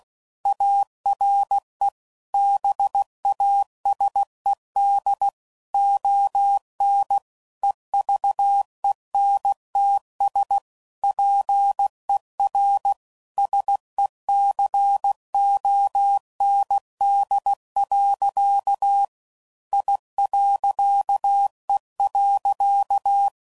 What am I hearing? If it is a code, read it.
Morse code, heard as a keyed tone, about 16 wpm, 800 Hz. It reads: ARE BASED ON EVENTS PER SECOND. I.E.
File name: CW Code Groups @15WPM / Technical Subjects / Reactance